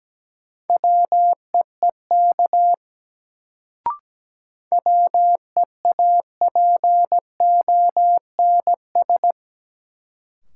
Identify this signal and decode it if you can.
Morse code, heard as a keyed tone, about 17 wpm, 700 Hz. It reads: WEEK WEAPONS
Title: Sets of 5 Words Encore - New Word Drill - 17wpm